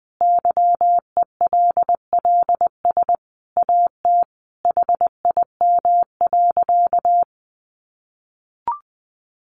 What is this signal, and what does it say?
Morse code, heard as a keyed tone, about 20 wpm, 700 Hz. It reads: YELLS AT HIM.
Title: Additional Easy Phrases 20wpm